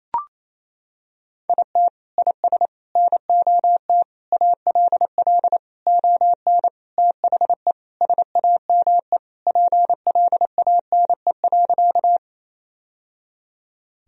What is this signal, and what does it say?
Morse code, heard as a keyed tone, about 27 wpm, 700 Hz. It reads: IT IS NOT ALL ON THE SAME PLANE.
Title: Sentences from Top 1000 Words - Review Part 3 - 28wpm